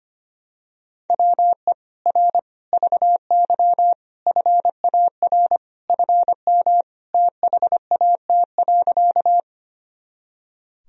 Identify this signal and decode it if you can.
Morse code, heard as a keyed tone, about 25 wpm, 700 Hz. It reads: WE R VY FAR FM THAT.
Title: Sentences from Top 300 Words and Abbreviations 25wpm